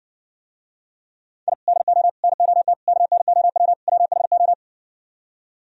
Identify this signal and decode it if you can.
Morse code, heard as a keyed tone, about 65 wpm, 700 Hz. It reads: I DO NOT KNOW WHO